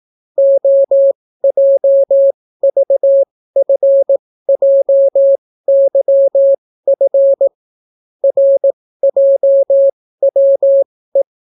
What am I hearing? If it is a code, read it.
Morse code, heard as a keyed tone, about 18 wpm, spaced as for 14 wpm, 550 Hz. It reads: OJVFJYF RJWE